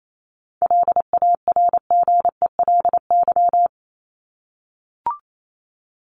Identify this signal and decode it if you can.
Morse code, heard as a keyed tone, about 28 wpm, 700 Hz. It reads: LARGELY
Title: LY Suffix - Top 100 Words with Base Words 28wpm